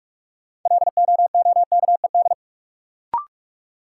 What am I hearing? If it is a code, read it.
Morse code, heard as a keyed tone, about 46 wpm, 700 Hz. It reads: LOOKED
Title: Sentences from Top 800 Words - New Word Drill - 45wpm